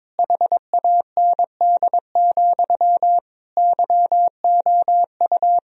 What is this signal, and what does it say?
Morse code, heard as a keyed tone, about 22 wpm, 700 Hz. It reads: HAND, YOU
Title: Sentences from Top 1000 Words - Review Part 3 - 22wpm